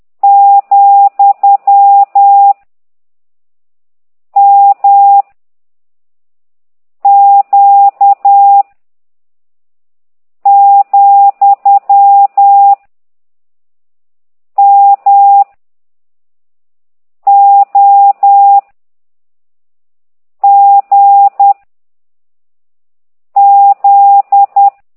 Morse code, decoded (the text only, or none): , M Q , M O G Z